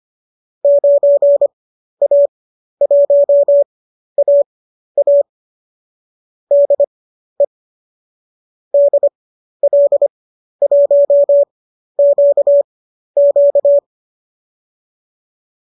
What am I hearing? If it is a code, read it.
Morse code, heard as a keyed tone, about 25 wpm, spaced as for 12 wpm, 590 Hz. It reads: 9A1AA DE DL1QQ
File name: qso012 12